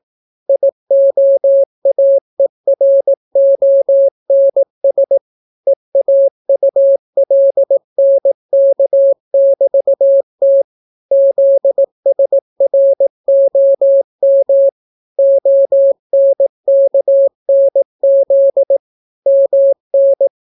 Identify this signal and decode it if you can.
Morse code, heard as a keyed tone, about 18 wpm, 550 Hz. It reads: IOAERONS EAULNK<BT>T ZSROM ONKNZ MN